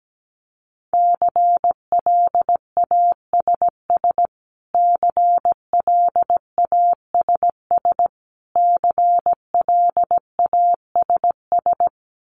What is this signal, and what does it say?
Morse code, heard as a keyed tone, about 17 wpm, 700 Hz. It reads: CLASS CLASS CLASS
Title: Top 500 Words Repeated 17wpm